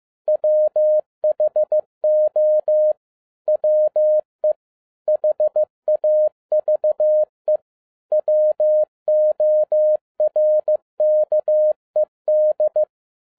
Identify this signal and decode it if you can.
Morse code, heard as a keyed tone, about 15 wpm, 610 Hz. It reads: WHO WE HAVE WORKED
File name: 2012-01-mp3 / n13-A NEWLY DESIGNED QSL CARD-15wpm